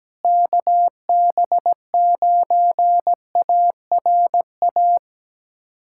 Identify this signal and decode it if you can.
Morse code, heard as a keyed tone, about 17 wpm, 700 Hz. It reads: KB9ARA
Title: USA Callsign DE Callsign - 6 Characters - 17wpm